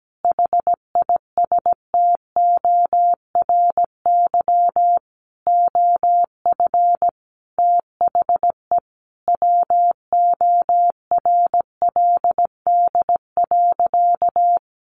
Morse code, heard as a keyed tone, about 17 wpm, 700 Hz. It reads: HISTORY OF THE WORLD.
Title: Sentences from Top 1100 Words 17wpm